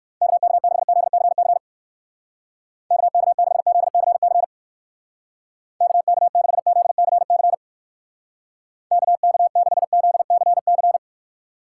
Morse code, heard as a keyed tone, about 56 wpm, 700 Hz. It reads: KK6ZCC KK6ZCC KK6ZCC KK6ZCC
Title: Speed Racing - USA 2x3 Call Signs 45wpm